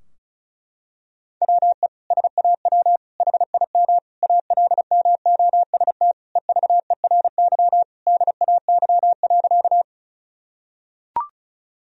Morse code, heard as a keyed tone, about 34 wpm, 700 Hz. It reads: WE SAW HIM ALMOST EVERY DAY.
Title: Sentences from Top 1000 Words - Review Part 4 - 35wpm